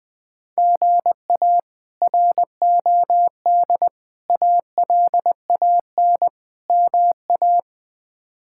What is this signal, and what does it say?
Morse code, heard as a keyed tone, about 20 wpm, 700 Hz. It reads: GA ROD ALAN MA
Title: K1USN SST QSO Practice - 20wpm